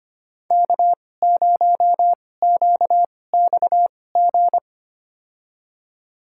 Morse code, heard as a keyed tone, about 25 wpm, spaced as for 18 wpm, 700 Hz. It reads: K0QXG
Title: USA Call Signs - 5 Characters - 2x Character Spacing - 25wpm